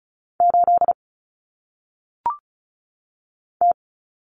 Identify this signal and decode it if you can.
Morse code, heard as a keyed tone, about 34 wpm, 700 Hz. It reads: 8 T
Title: Common Cut Numbers - 1-Digit 35wpm